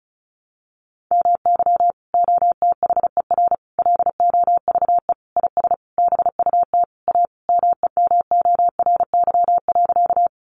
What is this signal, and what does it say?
Morse code, heard as a keyed tone, about 34 wpm, 700 Hz. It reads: MY OTHER LOVE IS BUT A MEMORY.
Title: Sentences from Top 1300 Words 35wpm